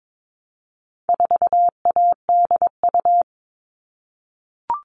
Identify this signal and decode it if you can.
Morse code, heard as a keyed tone, about 22 wpm, 700 Hz. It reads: 4ADU
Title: All Cut Numbers - 4-Digit 22wpm